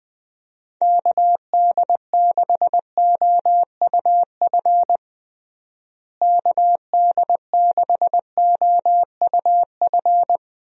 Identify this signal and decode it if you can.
Morse code, heard as a keyed tone, about 21 wpm, 700 Hz. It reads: KD6OUF KD6OUF
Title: USA Call Signs - 6 Characters - Repeated 3x - 20wpm